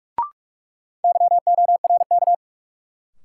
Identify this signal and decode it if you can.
Morse code, heard as a keyed tone, about 43 wpm, 700 Hz. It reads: YORK
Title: Sets of 2 Words - New Word Drill - 45wpm